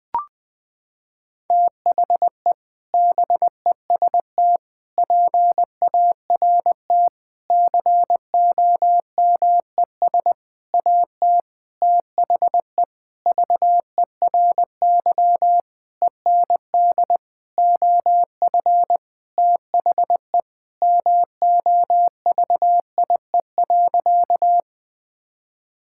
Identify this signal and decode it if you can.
Morse code, heard as a keyed tone, about 20 wpm, 700 Hz. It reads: THE BEST PART COMES AT THE VERY END OF THE MOVIE.
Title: Sentences with 6-Word Phrases - 20wpm